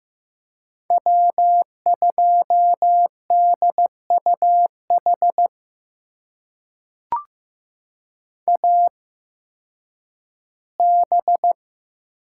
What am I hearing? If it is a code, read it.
Morse code, heard as a keyed tone, about 15 wpm, spaced as for 10 wpm, 700 Hz. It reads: W2DUH AB
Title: USA Call Signs - 5 Characters - 8x Character Spacing - 15wpm